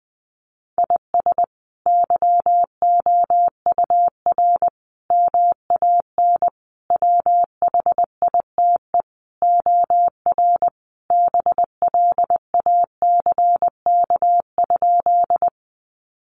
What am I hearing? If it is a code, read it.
Morse code, heard as a keyed tone, about 20 wpm, 700 Hz. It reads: IS YOUR MAN WHITE OR BLACK?
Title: Sentences from Top 400 Words 20wpm